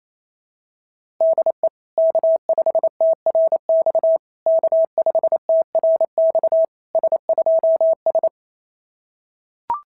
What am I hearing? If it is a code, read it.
Morse code, heard as a keyed tone, about 26 wpm, 650 Hz. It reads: DE K5TRX K5TRX S2S
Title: SOTA QSO Practice 28wpm